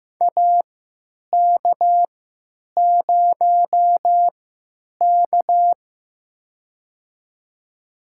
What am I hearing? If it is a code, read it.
Morse code, heard as a keyed tone, about 15 wpm, spaced as for 9 wpm, 700 Hz. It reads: AK0K
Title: USA Call Signs - 4 Characters - 3x Character Spacing - 15wpm